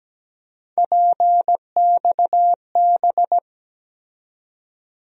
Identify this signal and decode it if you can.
Morse code, heard as a keyed tone, about 17 wpm, 700 Hz. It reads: PXB
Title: USA Call Sign Postfixes - 3 Letters 17wpm